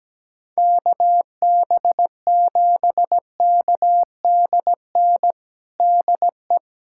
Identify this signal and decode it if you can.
Morse code, heard as a keyed tone, about 17 wpm, 700 Hz. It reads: KB7KDN DE
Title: USA Callsign DE Callsign - 6 Characters - 17wpm